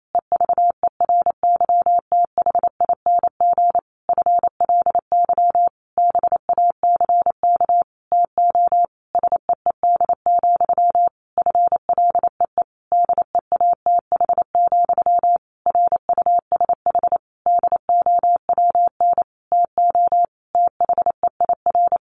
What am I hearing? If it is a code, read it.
Morse code, heard as a keyed tone, about 28 wpm, 700 Hz. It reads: EVERYTHING FLY BACK TO SEED, FLEE DEATH, RUSH DOWN TO THEIR
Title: A Sound Of Thunder 28wpm